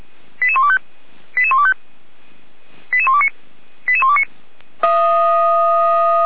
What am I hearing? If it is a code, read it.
Five-tone selective call (ZVEI), heard as quick runs of five tones, tones of about 70 ms. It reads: 88125 88125 88129 88129
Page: audio.f85360.feuerwehr-orlamuende.org